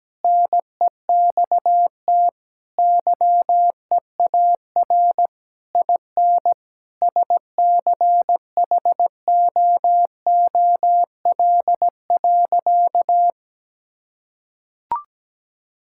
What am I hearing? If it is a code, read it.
Morse code, heard as a keyed tone, about 17 wpm, 700 Hz. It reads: NEXT YEAR IN SCHOOL.
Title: Sentences from Top 1000 Words - Review Part 5 - 17wpm